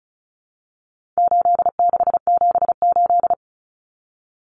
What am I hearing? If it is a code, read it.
Morse code, heard as a keyed tone, about 34 wpm, 700 Hz. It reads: 8678
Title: Numbers - 4-Digit 35wpm